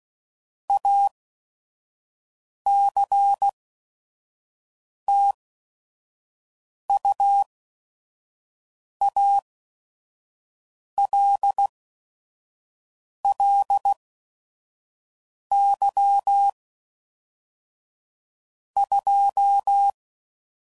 Morse code, heard as a keyed tone, about 16 wpm, spaced as for 5 wpm, 800 Hz. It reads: ACTUALLY 2
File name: CW Code Groups @5WPM